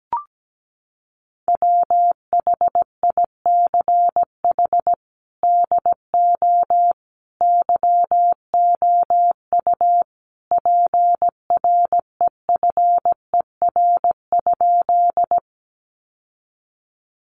Morse code, heard as a keyed tone, about 17 wpm, 700 Hz. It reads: WHICH DO YOU PREFER?
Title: Sentences from Top 1100 Words 17wpm